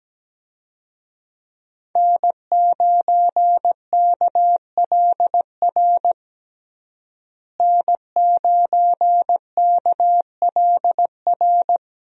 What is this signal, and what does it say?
Morse code, heard as a keyed tone, about 17 wpm, 700 Hz. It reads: N9KLR N9KLR
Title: USA Call Signs - 5 Characters - Repeated 3x - 17wpm